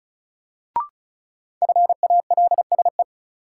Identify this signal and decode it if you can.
Morse code, heard as a keyed tone, about 36 wpm, 700 Hz. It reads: FALSE
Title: Sentences from Top 1600 Words - New Word Drill Alt - 35wpm